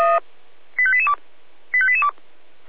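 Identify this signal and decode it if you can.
Five-tone selective call (ZVEI), heard as quick runs of five tones, tones of about 70 ms. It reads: 85882 85882
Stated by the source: ZVEI five-tone call 85243 received on an SDR.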